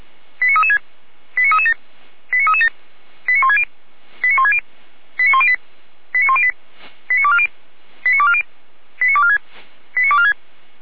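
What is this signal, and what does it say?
Five-tone selective call (ZVEI), heard as quick runs of five tones, tones of about 70 ms. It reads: 89336 89337 89337 89160 89160 89108 89108 89240 89240 89246 89246
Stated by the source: ZVEI five-tone call received by SDR.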